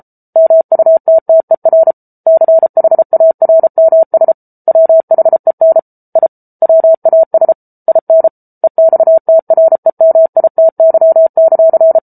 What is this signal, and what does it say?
Morse code, heard as a keyed tone, about 34 wpm, 650 Hz. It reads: MUTTER CHARMS WHEN I WAS IN EXTREMITY;